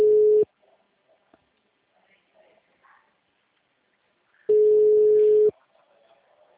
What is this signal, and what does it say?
Ringback tone (Europe, 425 Hz).